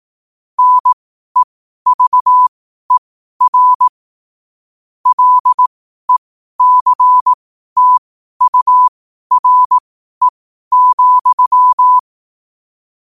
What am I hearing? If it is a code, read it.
Morse code, heard as a keyed tone, about 18 wpm, spaced as for 12 wpm, 1000 Hz. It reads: NEVER LECTURE,
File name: qotdfiles / 12wpm